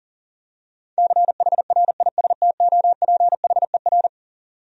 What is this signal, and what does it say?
Morse code, heard as a keyed tone, about 38 wpm, 700 Hz. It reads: CHRISTOPHER
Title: USA Common Names - 40wpm